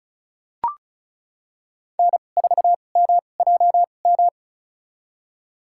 Morse code, heard as a keyed tone, about 34 wpm, spaced as for 25 wpm, 700 Hz. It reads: N4MJM
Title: USA Call Signs - 5 Characters - 2x Character Spacing - 35wpm